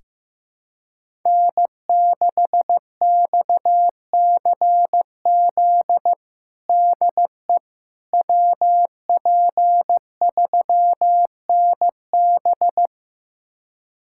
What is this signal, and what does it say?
Morse code, heard as a keyed tone, about 15 wpm, 700 Hz. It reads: N6XCZ DE WP3NB
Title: USA Callsign DE Callsign - 5 Characters - 15wpm